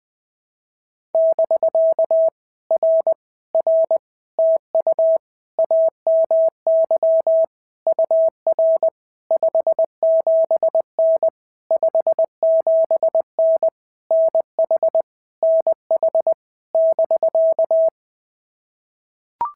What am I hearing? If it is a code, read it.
Morse code, heard as a keyed tone, about 20 wpm, 650 Hz. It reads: <BK> R R TU AMY UR 57N 57N NH NH <BK>
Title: POTA QSO Practice - 20wpm